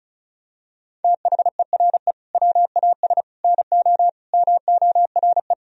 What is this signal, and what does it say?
Morse code, heard as a keyed tone, about 36 wpm, 700 Hz. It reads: THERE WAS NO MORE
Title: Sentences from Top 1000 Words - Review Part 4 - 35wpm